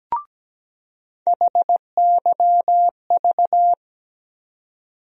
Morse code, heard as a keyed tone, about 17 wpm, 700 Hz. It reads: HYV
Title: USA Call Sign Postfixes - 3 Letters 17wpm